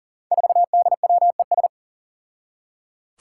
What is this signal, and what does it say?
Morse code, heard as a keyed tone, about 38 wpm, 700 Hz. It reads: 4DWES
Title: Five Challenging Letter-Number Combinations 40wpm